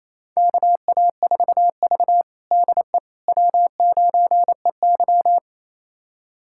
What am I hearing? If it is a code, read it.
Morse code, heard as a keyed tone, about 28 wpm, 700 Hz. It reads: KA4V DE W9EY
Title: USA Callsign DE Callsign - 4 Characters - 28wpm